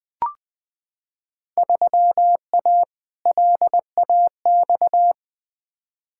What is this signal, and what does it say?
Morse code, heard as a keyed tone, about 21 wpm, 700 Hz. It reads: 3A LAX
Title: Field Day Exchanges - 20wpm